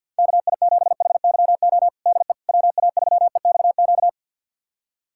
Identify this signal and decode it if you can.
Morse code, heard as a keyed tone, about 53 wpm, 700 Hz. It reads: KI7HYO DE WA3EXQ
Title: USA Callsign DE Callsign - 6 Characters - 50wpm